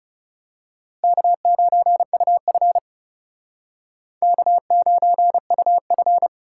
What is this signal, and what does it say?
Morse code, heard as a keyed tone, about 34 wpm, 700 Hz. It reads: K9UF K9UF
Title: Speed Racing - USA 1x2 Call Signs 30wpm